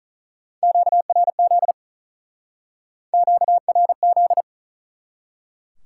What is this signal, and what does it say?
Morse code, heard as a keyed tone, about 47 wpm, 700 Hz. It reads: QRZ QRZ